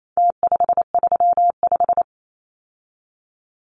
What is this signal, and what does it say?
Morse code, heard as a keyed tone, about 28 wpm, 700 Hz. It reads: T535
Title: Numbers - Contest Style - 4-Digit 28wpm